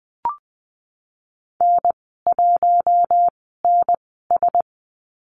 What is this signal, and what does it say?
Morse code, heard as a keyed tone, about 20 wpm, spaced as for 15 wpm, 700 Hz. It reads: N1NS
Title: 042 Lesson 8g - Number 1 - Call Signs 20wpm